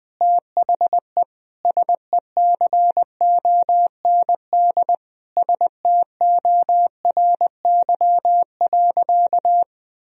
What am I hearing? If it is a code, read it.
Morse code, heard as a keyed tone, about 20 wpm, 700 Hz. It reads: THE SECOND STORY.
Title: Sentences from Top 1000 Words - Review Part 3 - 20wpm